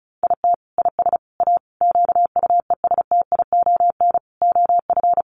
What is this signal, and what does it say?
Morse code, heard as a keyed tone, about 34 wpm, 700 Hz. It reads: IT IS A QUESTION OF
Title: Sentences from Top 1100 Words 35wpm